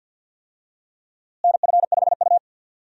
Morse code, heard as a keyed tone, about 51 wpm, 700 Hz. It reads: NF5U
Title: USA Call Signs - 4 Characters - Repeated 3x - 50wpm